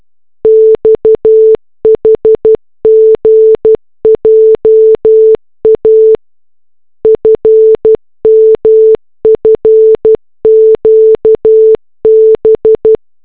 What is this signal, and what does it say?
Morse code, heard as a keyed tone, about 12 wpm, 440 Hz. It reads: XHGJA FMFQB